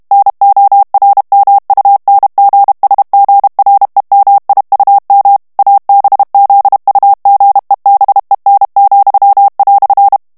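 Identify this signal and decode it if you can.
Morse code, heard as a keyed tone, about 32 wpm, 800 Hz. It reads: NORMUNGSGREMIUM ABZUGEBEN,"
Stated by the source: German plain-text Morse reading practice at 30 words per minute.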